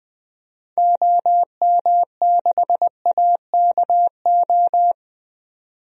Morse code, heard as a keyed tone, about 20 wpm, 700 Hz. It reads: OM6AKO
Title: DXCC-100 Call Signs - 6 Characters - 20wpm